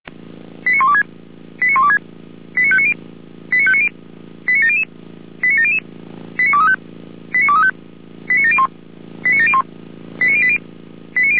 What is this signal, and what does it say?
Five-tone selective call (ZVEI), heard as quick runs of five tones, tones of about 70 ms. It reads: 89126 89126 89590 89590 89700 89700 89235 89235 89701 89701 89080
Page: /455/SDR/ZVEI/